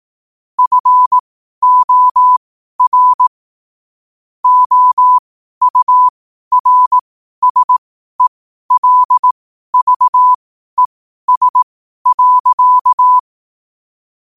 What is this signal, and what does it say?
Morse code, heard as a keyed tone, about 18 wpm, spaced as for 12 wpm, 1000 Hz. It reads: FOR OURSELVES.
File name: qotdfiles / 12wpm